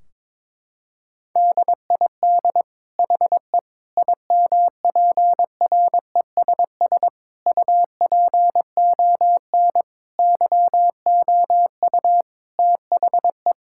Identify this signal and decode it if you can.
Morse code, heard as a keyed tone, about 21 wpm, 700 Hz. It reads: DID HE IMPRESS UPON YOU THE
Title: Sentences from Top 1600 Words 22wpm